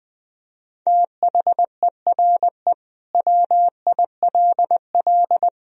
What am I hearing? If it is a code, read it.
Morse code, heard as a keyed tone, about 21 wpm, 700 Hz. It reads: THERE WILL
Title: Sentences from Top 1000 Words - Review Part 2 - 20wpm